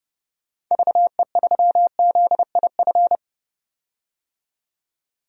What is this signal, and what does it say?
Morse code, heard as a keyed tone, about 29 wpm, 700 Hz. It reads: VE3ZIF
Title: DXCC-100 Call Signs - 6 Characters - 30wpm